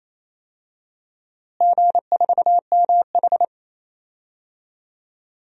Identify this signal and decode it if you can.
Morse code, heard as a keyed tone, about 28 wpm, 700 Hz. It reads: G4MH
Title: DXCC-100 Call Signs - 4 Characters - Repeated 2x - 28wpm